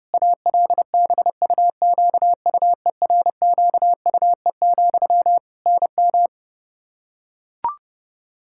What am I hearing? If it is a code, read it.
Morse code, heard as a keyed tone, about 32 wpm, 700 Hz. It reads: ALBUQUERQUE, NM
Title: Practice QSO - Exchange 1 - Beginning - 30wpm